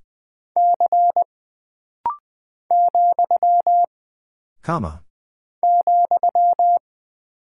Morse code, heard as a keyed tone, about 20 wpm, 700 Hz. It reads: C , ,
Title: Single Character (Letter:Number:Punctuation:Prosign) - Rapid-Fire - 20wpm